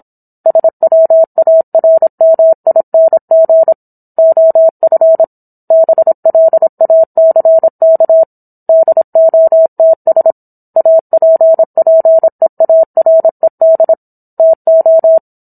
Morse code, heard as a keyed tone, about 27 wpm, 650 Hz. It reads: SWARMING OF BLACK DOTS APPEARED TO
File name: Book One / 17-Chapter 17 - 26 WPM